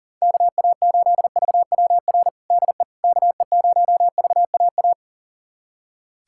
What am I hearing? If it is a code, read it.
Morse code, heard as a keyed tone, about 40 wpm, 700 Hz. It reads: KA8VWR DE KE0VAA
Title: USA Callsign DE Callsign - 6 Characters - 40wpm